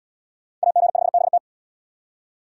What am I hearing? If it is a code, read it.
Morse code, heard as a keyed tone, about 67 wpm, 700 Hz. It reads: AG5BT